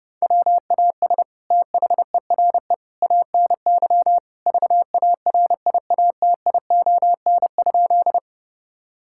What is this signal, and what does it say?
Morse code, heard as a keyed tone, about 30 wpm, 700 Hz. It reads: WAS THERE ANY VARIATION?
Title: Sentences from Top 1600 Words 30wpm